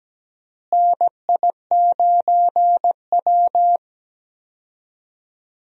Morse code, heard as a keyed tone, about 17 wpm, 700 Hz. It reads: NI9W
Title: USA Call Signs - 4 Characters - Repeated 3x - 17wpm